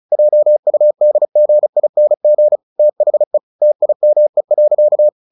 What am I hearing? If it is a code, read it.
Morse code, heard as a keyed tone, about 34 wpm, 600 Hz. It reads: JUDGING THE TIME.
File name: the old man and the sea 2 35wpm